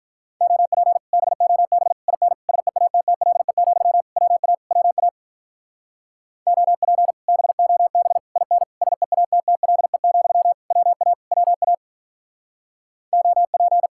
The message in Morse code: OP BOB IN SEATTLE, WA WA OP BOB IN SEATTLE, WA WA OP

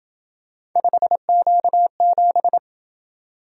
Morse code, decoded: 5Q7

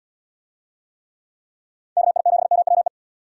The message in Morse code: KI7MWE

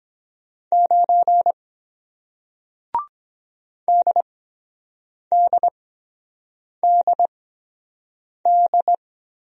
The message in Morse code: 9 DDDD

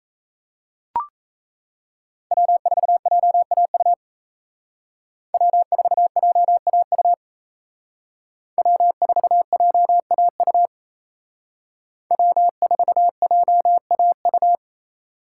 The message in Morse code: W4JAU W4JAU W4JAU W4JAU

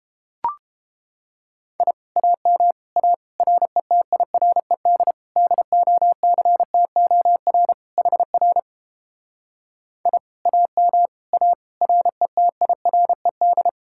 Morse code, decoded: I AM A RETIRED DOCTOR HR I AM A RETIRED

